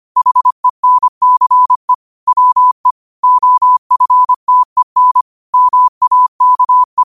Morse code, heard as a keyed tone, about 25 wpm, 1000 Hz. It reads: HENCE WE OFTEN MAKE
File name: qotdfiles / 25wpm